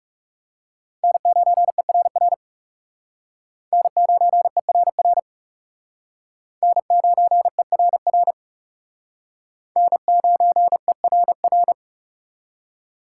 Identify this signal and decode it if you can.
Morse code, heard as a keyed tone, about 39 wpm, 700 Hz. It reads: N9ERR N9ERR N9ERR N9ERR